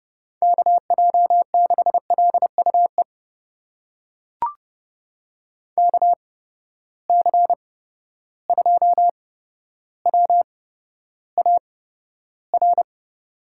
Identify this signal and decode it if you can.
Morse code, heard as a keyed tone, about 32 wpm, spaced as for 13 wpm, 700 Hz. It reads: KJ6LUE KC2WAR
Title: USA Call Signs - 6 Characters - 8x Character Spacing - 30wpm